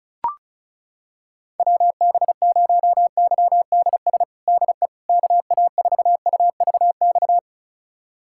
Morse code, heard as a keyed tone, about 34 wpm, 700 Hz. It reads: WB0YDS DE KA4UVX
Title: USA Callsign DE Callsign - 6 Characters - 35wpm